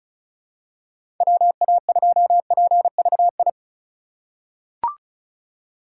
Morse code, heard as a keyed tone, about 34 wpm, 700 Hz. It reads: WA2PVI
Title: USA Call Signs - 6 Characters - 3x Character Spacing - 35wpm